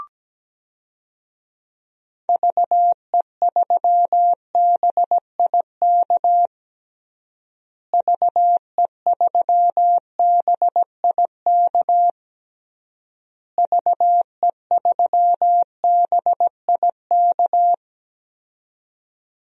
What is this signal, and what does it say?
Morse code, heard as a keyed tone, about 17 wpm, 700 Hz. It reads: VE3BIK VE3BIK VE3BIK
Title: DXCC-100 Call Signs - 6 Characters - Repeated 3x - 17wpm